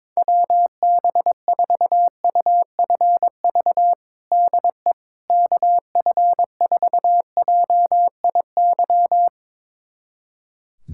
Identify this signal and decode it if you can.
Morse code, heard as a keyed tone, about 22 wpm, 700 Hz. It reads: WB4UFV DE KF4JIY
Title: USA Callsign DE Callsign - 6 Characters - 22wpm